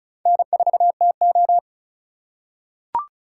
Morse code, heard as a keyed tone, about 36 wpm, 700 Hz. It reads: N4TO
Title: USA Call Signs - 4 Characters - 4x Character Spacing - 35wpm